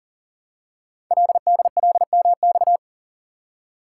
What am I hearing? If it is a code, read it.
Morse code, heard as a keyed tone, about 39 wpm, 690 Hz. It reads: LDLMX
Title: Code Groups - 5 Letters - 40wpm